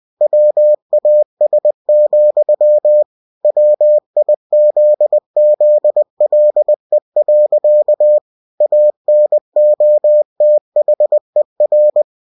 Morse code, heard as a keyed tone, about 20 wpm, 600 Hz. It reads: WAS, WIZZLE. ANOTHER
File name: cw-books / winnie-the-pooh / winnie-20wpm-page-0023